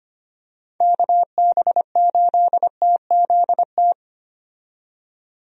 Morse code, heard as a keyed tone, about 25 wpm, 700 Hz. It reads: KB8TZT
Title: USA Call Signs - 6 Characters - 25wpm